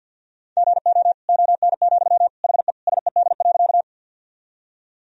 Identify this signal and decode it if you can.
Morse code, heard as a keyed tone, about 47 wpm, 700 Hz. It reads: GO ON, HE SED.
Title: Sentences from Top 100 Words and Abbreviations 50wpm